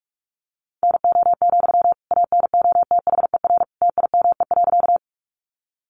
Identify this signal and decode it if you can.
Morse code, heard as a keyed tone, about 44 wpm, 700 Hz. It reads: NO, ANOTHER TIME.
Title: Sentences from Top 600 Words 45wpm